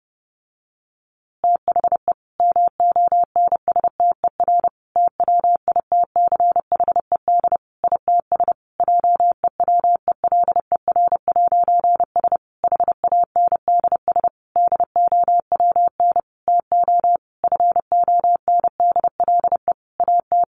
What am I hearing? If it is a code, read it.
Morse code, heard as a keyed tone, about 30 wpm, 700 Hz. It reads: THE MONSTER TWITCHED ITS JEWELER'S HANDS DOWN TO FONDLE AT
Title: A Sound Of Thunder 30wpm